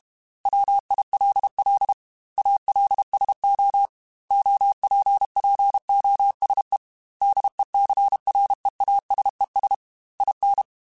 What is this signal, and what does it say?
Morse code, heard as a keyed tone, about 32 wpm, 800 Hz. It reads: WILL ALSO OPPOSE DECREASES IN